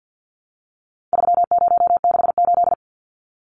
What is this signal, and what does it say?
Morse code, heard as a keyed tone, about 50 wpm, 700 Hz. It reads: 3068
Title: Numbers - 4-Digit 50wpm